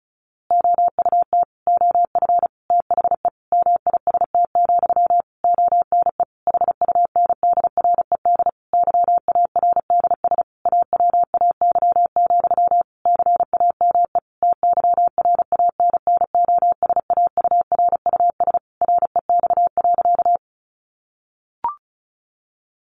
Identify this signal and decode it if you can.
Morse code, heard as a keyed tone, about 34 wpm, 700 Hz. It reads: OUT OF THE MIST, ONE HUNDRED YARDS AWAY, CAME TYRANNOSAURUS REX.
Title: A Sound Of Thunder 35wpm